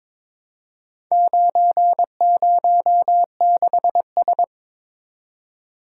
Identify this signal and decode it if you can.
Morse code, heard as a keyed tone, about 22 wpm, 690 Hz. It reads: 906S
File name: practice / 22wpm / Four Letter-Number Combinations - 22wpm